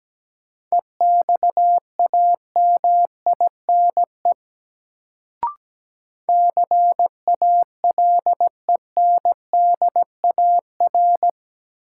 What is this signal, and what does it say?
Morse code, heard as a keyed tone, about 17 wpm, 700 Hz. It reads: EXAMINE CALENDAR
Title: Sentences from Top 1600 Words - New Word Drill Alt - 17wpm